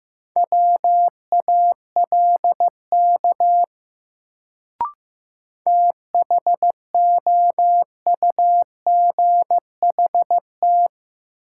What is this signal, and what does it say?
Morse code, heard as a keyed tone, about 15 wpm, 700 Hz. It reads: WALK THOUGHT
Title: Sentences from Top 300 Words - New Word Drill - 15wpm